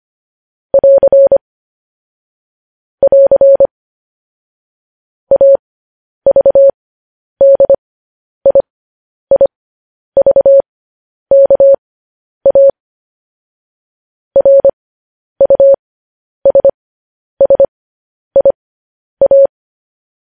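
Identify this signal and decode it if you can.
Morse code, heard as a keyed tone, about 25 wpm, spaced as for 10 wpm, 550 Hz. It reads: <AR> <AR> AVDIIVKA RUSSIA
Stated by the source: Morse code news headlines sent at 10 wpm for practice.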